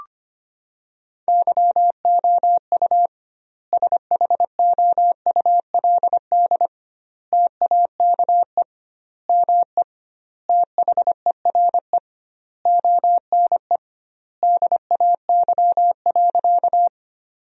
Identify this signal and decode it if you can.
Morse code, heard as a keyed tone, about 25 wpm, 700 Hz. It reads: YOU SHOULD TAKE ME THERE ONE DAY.
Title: 2x Word-Spacing - Sentences from Top 300 Words - 25wpm